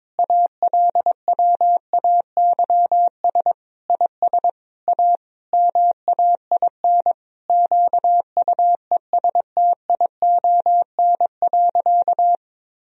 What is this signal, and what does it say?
Morse code, heard as a keyed tone, about 22 wpm, 700 Hz. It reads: ALWAYS IS A MAIN QUESTION.
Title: Sentences from Top 1000 Words - Review Part 2 - 22wpm